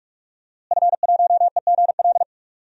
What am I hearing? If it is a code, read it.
Morse code, heard as a keyed tone, about 46 wpm, 700 Hz. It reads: F1EGL
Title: DXCC-100 Call Signs - 5 Characters - Repeated 2x - 45wpm